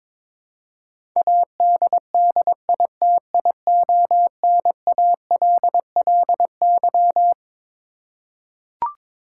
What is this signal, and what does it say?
Morse code, heard as a keyed tone, about 22 wpm, 700 Hz. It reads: ADDITIONALLY
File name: Twelve Letter Words - 22wpm